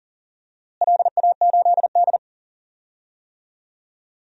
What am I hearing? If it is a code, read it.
Morse code, heard as a keyed tone, about 40 wpm, 690 Hz. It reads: LA8D
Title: DXCC-100 Call Signs - 4 Characters - Repeated 2x - 40wpm